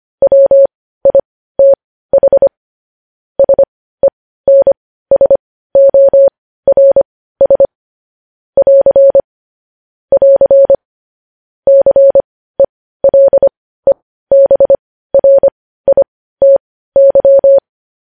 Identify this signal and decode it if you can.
Morse code, heard as a keyed tone, about 25 wpm, spaced as for 15 wpm, 550 Hz. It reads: WITH SENSORS <AR> <AR> CELEBRITY
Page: News Headlines in Morse Code at 15 WPM